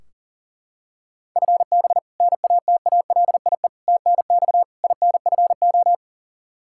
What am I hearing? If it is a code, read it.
Morse code, heard as a keyed tone, about 37 wpm, 700 Hz. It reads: FB NATALIE TNX INFO
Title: Practice QSO - Exchange 3 - More Info - 40wpm